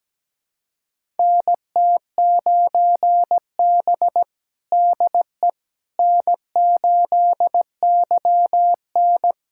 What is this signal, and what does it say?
Morse code, heard as a keyed tone, about 17 wpm, 700 Hz. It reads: NT9B DE N8YN
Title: USA Callsign DE Callsign - 4 Characters - 17wpm